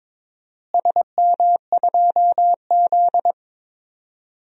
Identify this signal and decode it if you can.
Morse code, heard as a keyed tone, about 22 wpm, 700 Hz. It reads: SM2Z
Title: DXCC-100 Call Signs - 4 Characters - Repeated 3x - 22wpm